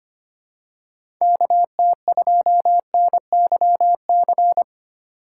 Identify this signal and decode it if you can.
Morse code, heard as a keyed tone, about 25 wpm, 700 Hz. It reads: KT2NYC
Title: USA Call Signs - 6 Characters - 6x Character Spacing - 25wpm